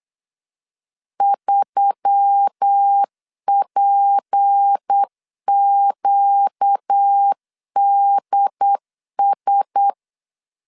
Morse code, heard as a keyed tone, about 8 wpm, 800 Hz. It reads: 3PQDS
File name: LEZIONI / 6 PKX0 / 6 PK 8wpm 3003